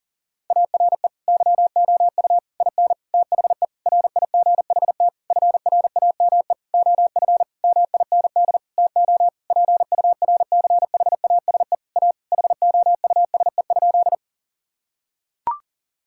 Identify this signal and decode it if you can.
Morse code, heard as a keyed tone, about 40 wpm, 700 Hz. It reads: ARE YOU IN THE RIGHT FRAME OF MIND TO PURCHASE A HOUSE?
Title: Sentences from Top 1000 Words - Review Part 6 - 40wpm